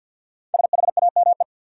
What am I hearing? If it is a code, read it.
Morse code, heard as a keyed tone, about 47 wpm, 700 Hz. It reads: SHAME
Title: Five Letter Words - 50wpm